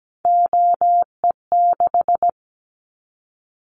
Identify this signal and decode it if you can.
Morse code, heard as a keyed tone, about 17 wpm, 700 Hz. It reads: OE6
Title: Callsign Prefixes for Top 25 DXCC - Repeated 3x - 17wpm